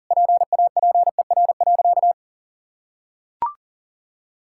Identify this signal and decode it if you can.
Morse code, heard as a keyed tone, about 40 wpm, 700 Hz. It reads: PAPER.
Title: Sentences from Top 1600 Words 40wpm